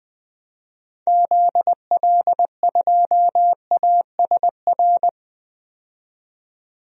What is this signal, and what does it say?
Morse code, heard as a keyed tone, about 20 wpm, 700 Hz. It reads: ZL2ASR
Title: DXCC-100 Call Signs - 6 Characters - 20wpm